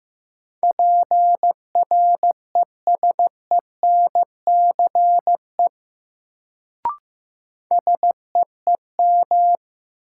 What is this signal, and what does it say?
Morse code, heard as a keyed tone, about 15 wpm, 700 Hz. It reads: PRESENCE SEEM
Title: Sets of 3 Words - New Word Drill - 15wpm